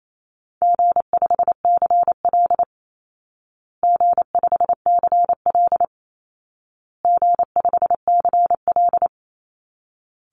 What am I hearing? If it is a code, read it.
Morse code, heard as a keyed tone, about 28 wpm, 700 Hz. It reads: G5CL G5CL G5CL
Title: Call Signs Repeated 2x - 28wpm